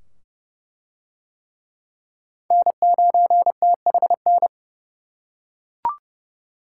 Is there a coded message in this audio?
Yes: Morse code from 2.5 to 4.5 s, heard as a keyed tone, about 29 wpm, 700 Hz. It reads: N9THN